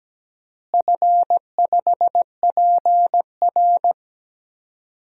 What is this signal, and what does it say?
Morse code, heard as a keyed tone, about 17 wpm, 700 Hz. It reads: F5PR
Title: DXCC-100 Call Signs - 4 Characters - Repeated 3x - 17wpm